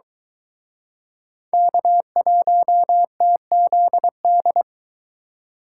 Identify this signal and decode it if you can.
Morse code, heard as a keyed tone, about 23 wpm, 700 Hz. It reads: K1TZD